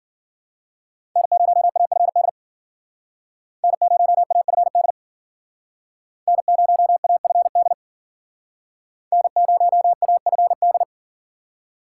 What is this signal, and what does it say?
Morse code, heard as a keyed tone, about 53 wpm, 700 Hz. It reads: N0AFD N0AFD N0AFD N0AFD